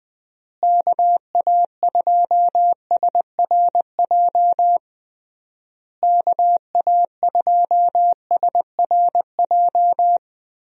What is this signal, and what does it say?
Morse code, heard as a keyed tone, about 20 wpm, 700 Hz. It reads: KA2SRJ KA2SRJ
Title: USA Call Signs - 6 Characters - Repeated 3x - 20wpm